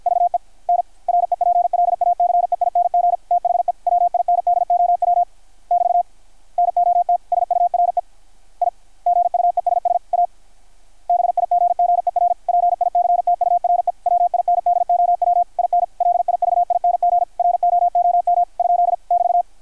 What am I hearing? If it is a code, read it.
Morse code, heard as a keyed tone, about 53 wpm, 660 Hz, with radio noise. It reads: VE N REPLACEING THE WINDOW <BT> NOT SURE I GUESS A BIGGER PICTURE WINDOW IN LIVING ROOM ? <BT>
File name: archived / CRSnet